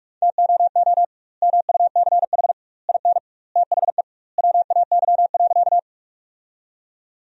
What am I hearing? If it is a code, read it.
Morse code, heard as a keyed tone, about 46 wpm, 700 Hz. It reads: TOO MUCH IN THE WAY.
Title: 2x Word-Spacing - Sentences from Top 200 Words - 45wpm